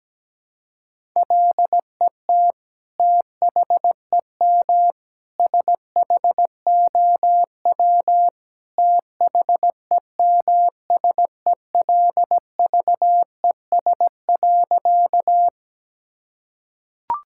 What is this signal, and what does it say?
Morse code, heard as a keyed tone, about 17 wpm, 700 Hz. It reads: LET THEM SHOW THEMSELVES.
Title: Sentences from Top 1000 Words - Review Part 5 - 17wpm